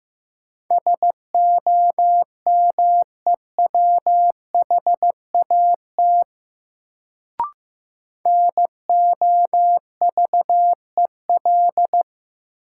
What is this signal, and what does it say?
Morse code, heard as a keyed tone, about 15 wpm, 700 Hz. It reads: SOMEWHAT NOVEL
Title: Sentences from Top 1300 Words - New Word Drill Alt - 15wpm